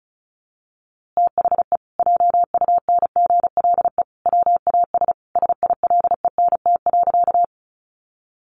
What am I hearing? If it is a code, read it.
Morse code, heard as a keyed tone, about 34 wpm, 700 Hz. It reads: THE JUNGLE WAS SILENT.
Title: A Sound Of Thunder 35wpm